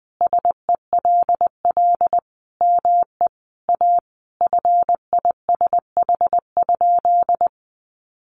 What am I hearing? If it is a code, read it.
Morse code, heard as a keyed tone, about 20 wpm, 700 Hz. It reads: SELL ME A FISH?
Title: Sentences from Top 1100 Words 20wpm